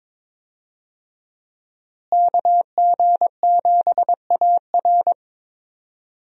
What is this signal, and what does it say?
Morse code, heard as a keyed tone, about 22 wpm, 700 Hz. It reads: KG7AR
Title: USA Call Signs - 5 Characters - Repeated 3x - 22wpm